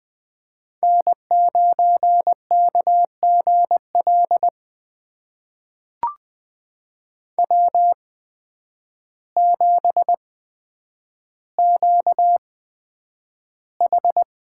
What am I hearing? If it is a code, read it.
Morse code, heard as a keyed tone, about 20 wpm, spaced as for 9 wpm, 700 Hz. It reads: N9KGL W7QH
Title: USA Call Signs - 8x Character Spacing - 20wpm